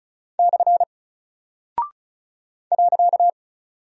Morse code, heard as a keyed tone, about 36 wpm, 700 Hz. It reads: / .
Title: Single Character (Letter:Number:Punctuation:Prosign) Alt 35wpm